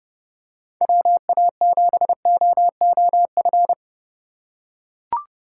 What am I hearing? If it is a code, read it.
Morse code, heard as a keyed tone, about 29 wpm, 700 Hz. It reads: WA7OOF